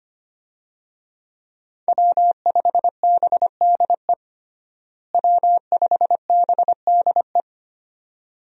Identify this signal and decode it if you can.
Morse code, heard as a keyed tone, about 25 wpm, 700 Hz. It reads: W5BDE W5BDE
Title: USA Call Signs - 5 Characters - Repeated 2x - 25wpm